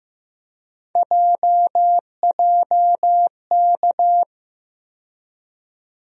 Morse code, heard as a keyed tone, about 15 wpm, 700 Hz. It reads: JJK
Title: USA Call Sign Postfixes - 3 Letters 15wpm